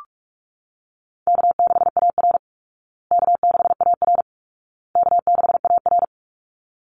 Morse code, heard as a keyed tone, about 44 wpm, 700 Hz. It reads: K6AR K6AR K6AR
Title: Call Signs Repeated 2x - 45wpm